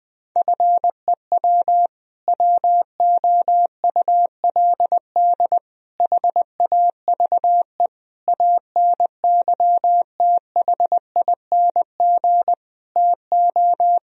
Morse code, heard as a keyed tone, about 20 wpm, 700 Hz. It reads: FEW WOULD HAVE ANYTHING TO
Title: Sentences from Top 1000 Words - Review Part 1 - 20wpm